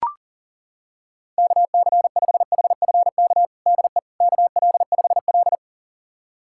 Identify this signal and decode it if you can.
Morse code, heard as a keyed tone, about 40 wpm, 700 Hz. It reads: KC5HFK DE KL5L
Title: USA Callsign DE Callsign 40wpm